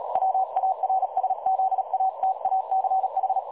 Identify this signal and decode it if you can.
Morse code, heard as a keyed tone, about 37 wpm, 760 Hz. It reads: WARSZATACH
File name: dlazaawansowanych / nagraniazpasma / websdr recording 3565.0kHz